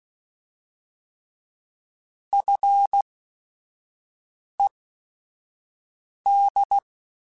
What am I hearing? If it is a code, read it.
Morse code, heard as a keyed tone, about 16 wpm, spaced as for 5 wpm, 800 Hz. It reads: FED
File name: Technical Subjects / Technical Subjects-mp3s / Sw Pwr Sup @5